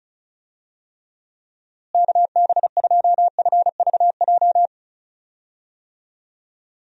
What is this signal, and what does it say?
Morse code, heard as a keyed tone, about 34 wpm, 700 Hz. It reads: KB2FVJ
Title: USA Call Signs - 6 Characters - Repeated 2x - 35wpm